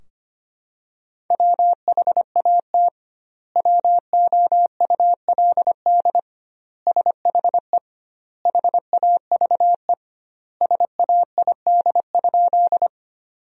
Morse code, heard as a keyed tone, about 25 wpm, 700 Hz. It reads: WHAT WOULD SHE HAVE SAID?
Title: 2x Word-Spacing - Sentences from Top 100 Words - 25wpm